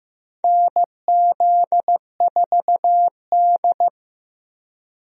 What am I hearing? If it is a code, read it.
Morse code, heard as a keyed tone, about 15 wpm, 700 Hz. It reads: NZ4D
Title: USA Call Signs - 4 Characters - Repeated 2x - 15wpm